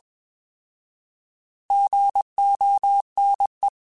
Morse code, heard as a keyed tone, about 20 wpm, 800 Hz. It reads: GONE
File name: Alphabetical / Alphabetical mp3s / Gs @20